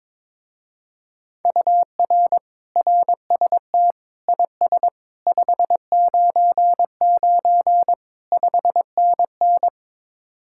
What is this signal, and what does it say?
Morse code, heard as a keyed tone, about 22 wpm, 700 Hz. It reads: UR RST IS 599 5NN
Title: Speed-Racing Common QSO Phrases - 22wpm